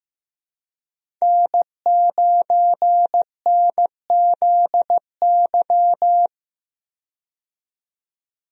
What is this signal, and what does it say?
Morse code, heard as a keyed tone, about 15 wpm, 700 Hz. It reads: N9NZY
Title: USA Call Signs - 5 Characters - 15wpm